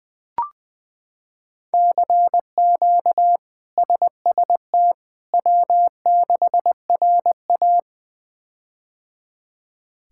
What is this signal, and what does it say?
Morse code, heard as a keyed tone, about 20 wpm, 700 Hz. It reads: CQ SST W6RA
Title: K1USN SST QSO Practice - 20wpm